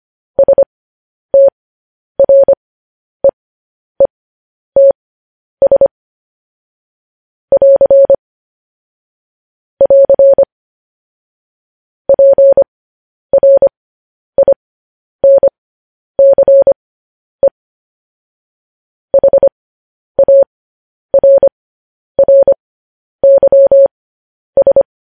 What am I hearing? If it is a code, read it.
Morse code, heard as a keyed tone, about 25 wpm, spaced as for 10 wpm, 550 Hz. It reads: STREETS <AR> <AR> PRINCE HARRYS
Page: News Headlines in Morse Code at 10 WPM